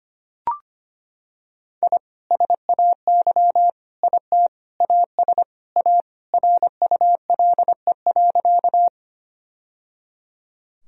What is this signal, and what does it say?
Morse code, heard as a keyed tone, about 25 wpm, 700 Hz. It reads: I SAY IT AS A RULE.
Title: Sentences from Top 1000 Words - Review Part 2 - 25wpm